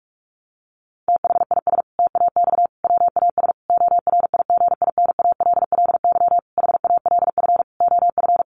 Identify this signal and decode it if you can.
Morse code, heard as a keyed tone, about 44 wpm, 700 Hz. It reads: THIS TAX WAS ORIGINALLY HALF OF
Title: Sentences from Top 1500 Words 45wpm